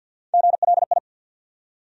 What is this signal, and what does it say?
Morse code, heard as a keyed tone, about 50 wpm, 700 Hz. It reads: GLI
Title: USA Call Sign Postfixes - 3 Letters 50wpm